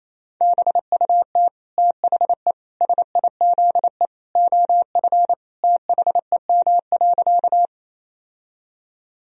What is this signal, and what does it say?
Morse code, heard as a keyed tone, about 28 wpm, 700 Hz. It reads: BUT THE SIZE OF THEM.
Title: Sentences from Top 1000 Words - Review Part 2 - 28wpm